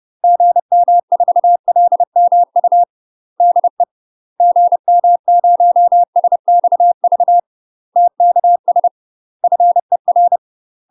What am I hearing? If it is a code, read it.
Morse code, heard as a keyed tone, about 30 wpm, 700 Hz. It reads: GM4LMU DE GM0SXV TKS FER
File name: UK MASTER QSOs TRAINING / 30WPM / UK MASTER QSOs 0039